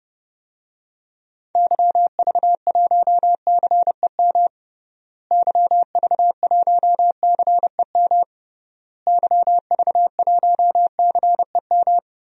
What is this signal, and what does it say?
Morse code, heard as a keyed tone, about 30 wpm, 700 Hz. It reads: YV1CEM YV1CEM YV1CEM
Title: DXCC-100 Call Signs - 6 Characters - Repeated 3x - 30wpm